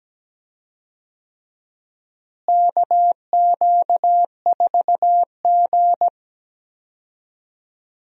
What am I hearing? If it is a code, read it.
Morse code, heard as a keyed tone, about 17 wpm, 700 Hz. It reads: KQ4G
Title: USA Call Signs - 4 Characters - Repeated 2x - 17wpm